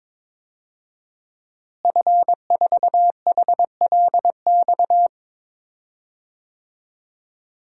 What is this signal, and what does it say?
Morse code, heard as a keyed tone, about 22 wpm, 700 Hz. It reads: F4HLX
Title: DXCC-100 Call Signs - 5 Characters - Repeated 2x - 22wpm